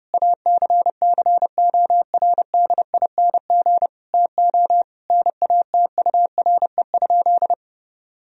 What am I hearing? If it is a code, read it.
Morse code, heard as a keyed tone, about 30 wpm, 700 Hz. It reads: ACCORDING TO NATURE?
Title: Sentences from Top 1000 Words - Review Part 4 - 30wpm